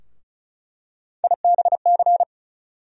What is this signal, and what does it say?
Morse code, heard as a keyed tone, about 36 wpm, 700 Hz. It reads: IBC
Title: USA Call Sign Postfixes - 3 Letters 35wpm